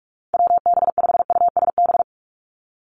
Morse code, heard as a keyed tone, about 46 wpm, 700 Hz. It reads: WB5USB
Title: NA SOTA Chaser Callsigns 45wpm